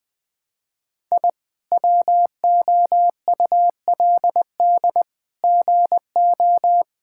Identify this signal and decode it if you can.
Morse code, heard as a keyed tone, about 20 wpm, 700 Hz. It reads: I WOULD GO
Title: Sets of 9 Words - 20wpm